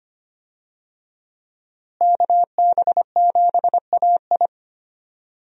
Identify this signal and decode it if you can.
Morse code, heard as a keyed tone, about 25 wpm, 700 Hz. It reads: KB7AI
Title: USA Call Signs - 5 Characters - Repeated 3x - 25wpm